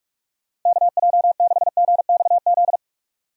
Morse code, heard as a keyed tone, about 46 wpm, 700 Hz. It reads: KJ6GXZ
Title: USA Call Signs - 6 Characters - 45wpm